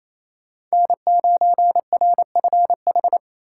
Morse code, heard as a keyed tone, about 34 wpm, 700 Hz. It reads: N9RFH
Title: Speed Racing - USA Call Signs 28wpm